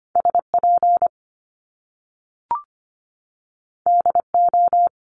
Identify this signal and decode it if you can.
Morse code, heard as a keyed tone, about 25 wpm, 700 Hz. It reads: SP DO
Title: Top Two Letter Combinations in Words 25wpm